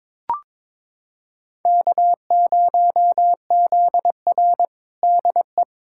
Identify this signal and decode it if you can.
Morse code, heard as a keyed tone, about 22 wpm, 700 Hz. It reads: K0ZR DE
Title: Practice QSO - Exchange 4 - Ending - 22wpm